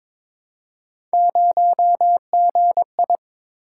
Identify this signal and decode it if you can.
Morse code, heard as a keyed tone, about 22 wpm, 700 Hz. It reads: 0GI
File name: practice / 22wpm / Three Letter-Number Combinations - 22wpm